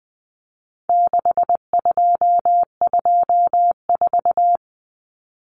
Morse code, heard as a keyed tone, about 20 wpm, 700 Hz. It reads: 6224
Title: Common Cut Numbers - 4-Digit 20wpm